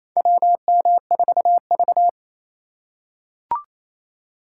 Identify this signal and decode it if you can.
Morse code, heard as a keyed tone, about 28 wpm, 700 Hz. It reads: WM4V
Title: USA Call Signs - 4 Characters - 4x Character Spacing - 28wpm